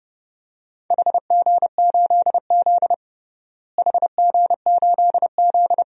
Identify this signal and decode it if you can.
Morse code, heard as a keyed tone, about 32 wpm, 700 Hz. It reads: HG8Z HG8Z
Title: DXCC-100 Call Signs - 4 Characters - Repeated 3x - 30wpm